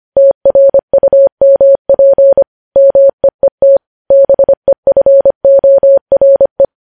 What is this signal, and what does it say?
Morse code, heard as a keyed tone, about 25 wpm, 550 Hz. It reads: TRUMP MEET BEFORE
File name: podcast / News headlines-1x-25x25